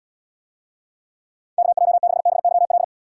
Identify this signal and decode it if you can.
Morse code, heard as a keyed tone, about 79 wpm, 700 Hz. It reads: KJ6BYL